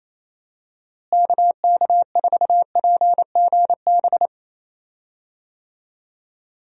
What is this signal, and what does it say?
Morse code, heard as a keyed tone, about 28 wpm, 700 Hz. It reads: KK4PGB
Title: USA Call Signs - 6 Characters - Repeated 2x - 28wpm